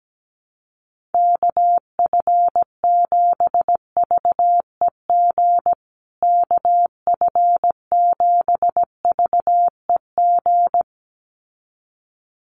Morse code, heard as a keyed tone, about 17 wpm, 700 Hz. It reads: KF7VEG KF7VEG
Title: USA Call Signs Repeated 2x 17wpm